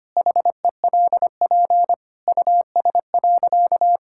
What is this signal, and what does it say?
Morse code, heard as a keyed tone, about 25 wpm, 700 Hz. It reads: HELP US.
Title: Sentences from Top 1000 Words - Review Part 5 - 25wpm